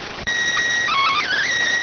SSTV (slow-scan television), PD 290.